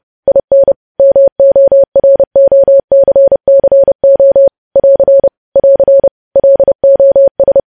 Morse code, heard as a keyed tone, about 30 wpm, 550 Hz. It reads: IN MOROCCO <AR> <AR> LOS
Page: news headlines in morse code at 30 wpm